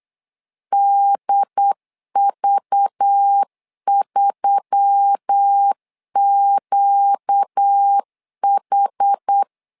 Morse code, heard as a keyed tone, about 8 wpm, 800 Hz. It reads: DV3QH
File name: LEZIONI / 5 LBVC / 5 LBV 8wpm 2837